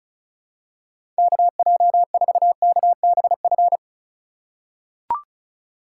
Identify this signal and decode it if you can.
Morse code, heard as a keyed tone, about 36 wpm, 700 Hz. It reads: KJ4KBF
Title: USA Call Signs - 4x Character Spacing - 35wpm